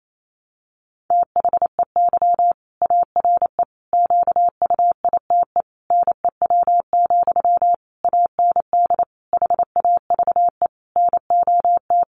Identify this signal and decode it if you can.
Morse code, heard as a keyed tone, about 28 wpm, 700 Hz. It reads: THEY ARE QUITE NEW, AND HAVE NOT
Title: Sentences from Top 700 Words 28wpm